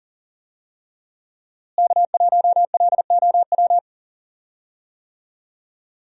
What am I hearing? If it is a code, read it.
Morse code, heard as a keyed tone, about 40 wpm, 700 Hz. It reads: K1LOW
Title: USA Call Signs - 5 Characters - Repeated 3x - 40wpm